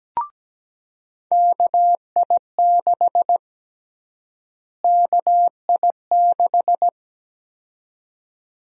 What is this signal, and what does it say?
Morse code, heard as a keyed tone, about 17 wpm, 700 Hz. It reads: KI6 KI6
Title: USA Call Sign Prefixes - Repeated 2x - 17wpm